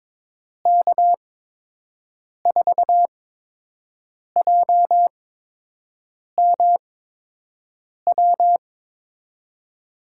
Morse code, heard as a keyed tone, about 22 wpm, spaced as for 6 wpm, 700 Hz. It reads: K4JMW
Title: USA Call Signs - 6 Characters - 8x Character Spacing - 22wpm